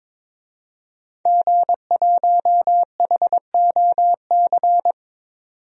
Morse code, heard as a keyed tone, about 22 wpm, 700 Hz. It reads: G1HOC
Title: DXCC-100 Call Signs - 5 Characters - 22wpm